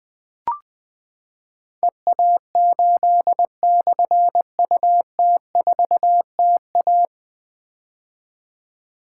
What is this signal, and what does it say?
Morse code, heard as a keyed tone, about 21 wpm, 700 Hz. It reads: EA8/UT4TA
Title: Difficult Call Signs 20wpm